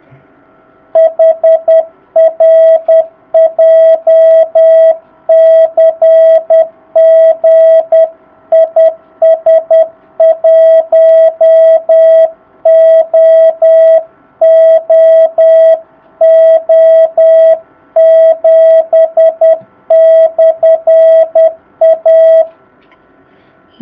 Morse code, heard as a keyed tone, about 10 wpm, 660 Hz. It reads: HR JCG IS 1OOO7/A
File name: jh8cbh / 2019cw